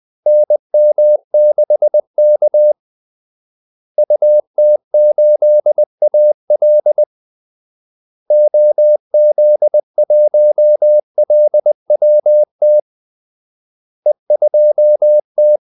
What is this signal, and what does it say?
Morse code, heard as a keyed tone, about 20 wpm, 600 Hz. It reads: NM6K UT8AL OZ1LWT E2T
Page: Kaldesignaler 20 wpm